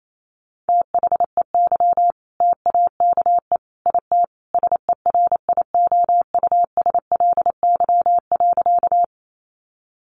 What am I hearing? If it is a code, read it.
Morse code, heard as a keyed tone, about 28 wpm, 700 Hz. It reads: THEY TAKE IT SERIOUSLY.
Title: Sentences from Top 1200 Words 28wpm